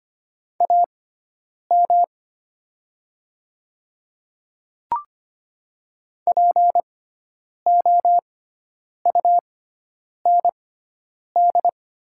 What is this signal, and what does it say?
Morse code, heard as a keyed tone, about 25 wpm, spaced as for 9 wpm, 700 Hz. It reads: AM POUND
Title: 6x Character Spacing - Top 500 Words - 25wpm